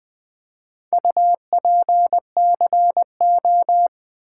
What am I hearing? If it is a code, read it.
Morse code, heard as a keyed tone, about 20 wpm, 700 Hz. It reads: UPCO
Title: Build A Word - 2 Letters At A Time - 8-Letter Words - 20wpm